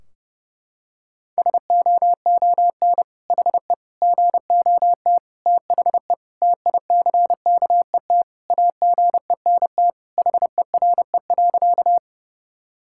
Morse code, heard as a keyed tone, about 30 wpm, 700 Hz. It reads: SOON HE GOT THE TICKET AGENT HERE.
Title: Sentences from Top 1600 Words 30wpm